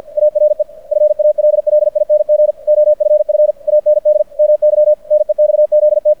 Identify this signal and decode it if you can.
Morse code, heard as a keyed tone, about 50 wpm, 600 Hz. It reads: ARE FALLING OFF AND MY NEXZT
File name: archived / CRSnet